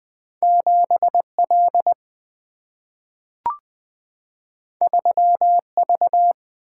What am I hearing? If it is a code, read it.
Morse code, heard as a keyed tone, about 20 wpm, 700 Hz. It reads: 7 L 3 V
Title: Single-Letter Single-Number 20wpm